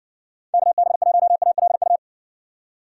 Morse code, heard as a keyed tone, about 61 wpm, 700 Hz. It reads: KB1ALU